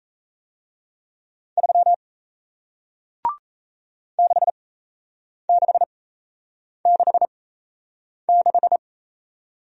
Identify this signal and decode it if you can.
Morse code, heard as a keyed tone, about 35 wpm, spaced as for 8 wpm, 700 Hz. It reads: 3 6666
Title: Speed-Racing Single Letter-Number 28wpm